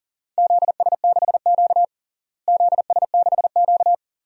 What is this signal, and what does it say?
Morse code, heard as a keyed tone, about 40 wpm, 700 Hz. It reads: ZS6Q ZS6Q
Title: DXCC-100 Call Signs - 4 Characters - Repeated 2x - 40wpm